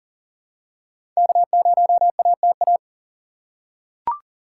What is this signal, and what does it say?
Morse code, heard as a keyed tone, about 40 wpm, 700 Hz. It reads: K0ATA
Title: USA Call Signs - 5 Characters - 2x Character Spacing - 40wpm